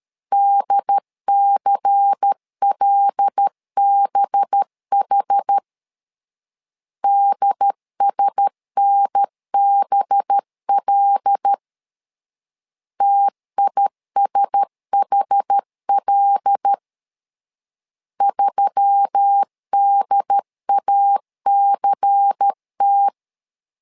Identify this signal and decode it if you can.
Morse code, heard as a keyed tone, about 13 wpm, 800 Hz. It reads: DCLBH DSNBL TISHL 3DACT
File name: LEZIONI / 5 LBVC / 5 LBVC 12wpm 1955